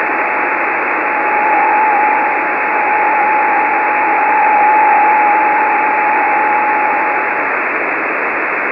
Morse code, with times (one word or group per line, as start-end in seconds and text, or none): none